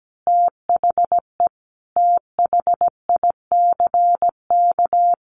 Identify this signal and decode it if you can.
Morse code, heard as a keyed tone, about 17 wpm, 700 Hz. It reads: THE THICK
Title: A Sound Of Thunder 17wpm